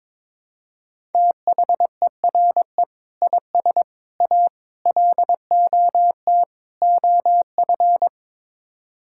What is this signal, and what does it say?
Morse code, heard as a keyed tone, about 22 wpm, 700 Hz. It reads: THERE IS A LOT OF
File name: practice / 22wpm / Build A Phrase - 5-Word Phrases - 22wpm